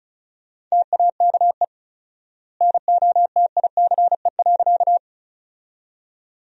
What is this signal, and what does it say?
Morse code, heard as a keyed tone, about 36 wpm, 700 Hz. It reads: TAKE NOTICE.
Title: 4x Word-Spacing - Sentences from Top 400 Words - 35wpm